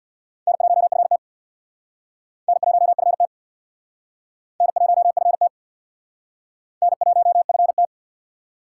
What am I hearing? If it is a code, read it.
Morse code, heard as a keyed tone, about 59 wpm, 700 Hz. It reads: N1FT N1FT N1FT N1FT
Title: Speed Racing - USA 1x2 Call Signs 50wpm